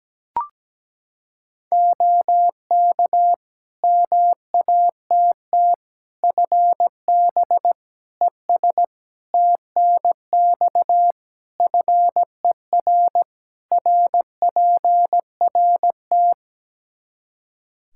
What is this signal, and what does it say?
Morse code, heard as a keyed tone, about 17 wpm, 700 Hz. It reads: OK MATT FB ES TNX FER RPRT
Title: Practice QSO - Exchange 2 - Info - 17wpm